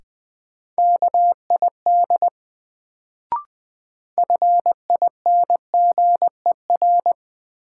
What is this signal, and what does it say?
Morse code, heard as a keyed tone, about 20 wpm, 700 Hz. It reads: KID FINGER
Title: Sentences from Top 1600 Words - New Word Drill - 20wpm